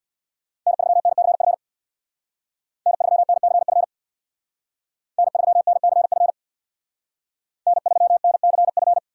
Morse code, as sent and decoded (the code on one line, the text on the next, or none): -. ...-- -. -.-. ..-. / -. ...-- -. -.-. ..-. / -. ...-- -. -.-. ..-. / -. ...-- -. -.-. ..-.
N3NCF N3NCF N3NCF N3NCF